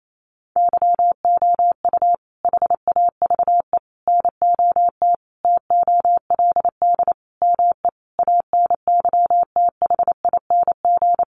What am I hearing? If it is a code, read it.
Morse code, heard as a keyed tone, about 28 wpm, 700 Hz. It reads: YOU HAVE NOT TOLD ME ANYTHING
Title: Sentences from Top 1000 Words 28wpm